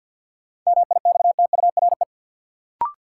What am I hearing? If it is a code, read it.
Morse code, heard as a keyed tone, about 53 wpm, 700 Hz. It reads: MIXTURE
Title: Sentences from Top 1500 Words - New Word Drill Alt - 50wpm